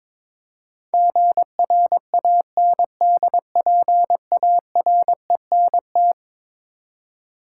Morse code, heard as a keyed tone, about 22 wpm, 700 Hz. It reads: GRANDPARENT
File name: practice / 22wpm / Eleven Letter Words - 22wpm